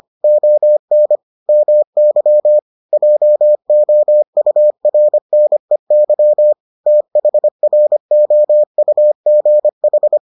Morse code, heard as a keyed tone, about 25 wpm, 600 Hz. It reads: ON MY JOURNEY THROUGH